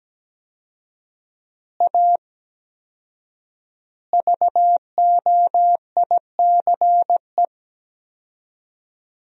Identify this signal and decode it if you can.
Morse code, heard as a keyed tone, about 17 wpm, 700 Hz. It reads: A VOICE
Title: 4x Word-Spacing - Sentences from Top 500 Words - 17wpm